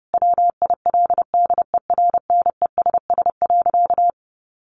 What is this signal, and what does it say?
Morse code, heard as a keyed tone, about 28 wpm, 700 Hz. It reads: WILDERNESS.
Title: A Sound Of Thunder 30wpm